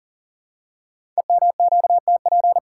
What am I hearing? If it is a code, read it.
Morse code, heard as a keyed tone, about 38 wpm, 690 Hz. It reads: EMQTP
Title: Code Groups - 5 Letters - 40wpm